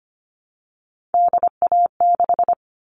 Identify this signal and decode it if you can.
Morse code, heard as a keyed tone, about 25 wpm, 700 Hz. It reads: DA6
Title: All Cut Numbers - 3-Digit 25wpm